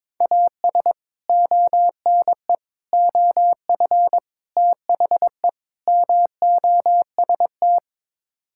Morse code, heard as a keyed tone, about 22 wpm, 700 Hz. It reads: AS ONE OF THE MOST
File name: Build A Phrase - 5-Word Phrases - 22wpm